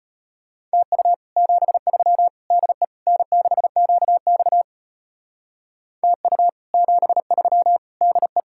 Morse code, heard as a keyed tone, about 33 wpm, 700 Hz. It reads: TU 73 DE N6QX TU 73 DE